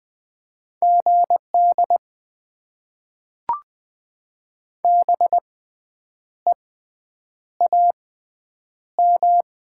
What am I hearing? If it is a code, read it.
Morse code, heard as a keyed tone, about 20 wpm, spaced as for 8 wpm, 700 Hz. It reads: GD BEAM
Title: Top 90 QSO Elements - 6x Character Spacing - 20wpm